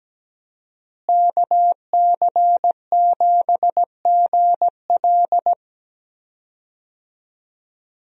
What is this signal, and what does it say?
Morse code, heard as a keyed tone, about 17 wpm, 700 Hz. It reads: KC7GL
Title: USA Call Signs - 5 Characters - Repeated 2x - 17wpm